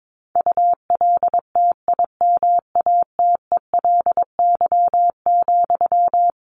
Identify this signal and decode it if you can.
Morse code, heard as a keyed tone, about 22 wpm, 700 Hz. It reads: ULTIMATELY,
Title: Sentences from Top 1200 Words 22wpm